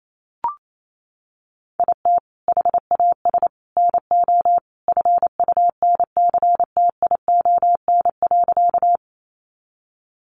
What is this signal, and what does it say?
Morse code, heard as a keyed tone, about 28 wpm, 700 Hz. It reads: IT HAS NO FUNCTION.
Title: Sentences from Top 1100 Words 28wpm